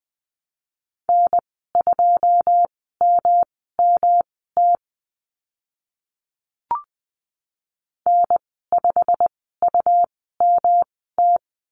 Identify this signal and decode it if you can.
Morse code, heard as a keyed tone, about 20 wpm, spaced as for 14 wpm, 700 Hz. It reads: N2MMT N5UMT
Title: 140 Lesson 23g - Letter f - Call Signs 20wpm